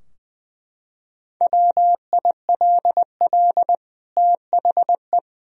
Morse code, heard as a keyed tone, about 20 wpm, 700 Hz. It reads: WILL THE
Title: Sentences from Top 1000 Words - Review Part 2 - 20wpm